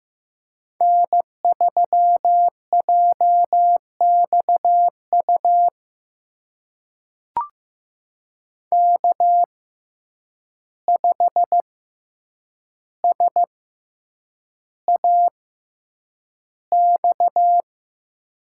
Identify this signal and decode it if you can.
Morse code, heard as a keyed tone, about 15 wpm, spaced as for 8 wpm, 700 Hz. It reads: N3JXU K5SAX
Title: USA Call Signs - 5 Characters - 6x Character Spacing - 15wpm